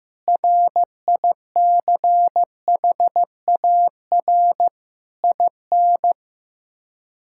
Morse code, heard as a keyed tone, about 15 wpm, 700 Hz. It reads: RICHAR IN
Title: K1USN SST QSO Practice - 15wpm